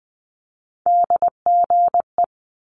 Morse code, heard as a keyed tone, about 20 wpm, 700 Hz. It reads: DGE
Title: Multi-letter Phonemes Quick Fire 20wpm